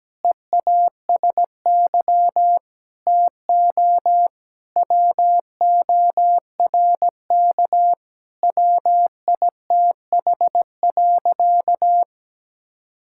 Morse code, heard as a keyed tone, about 17 wpm, 700 Hz. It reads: EASY TO WORK WITH.
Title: Sentences from Top 1000 Words - Review Part 6 - 17wpm